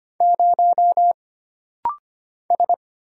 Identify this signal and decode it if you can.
Morse code, heard as a keyed tone, about 25 wpm, 700 Hz. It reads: 0 S
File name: Single Character (Letter:Number:Punctuation:Prosign) - Rapid-Fire - 25wpm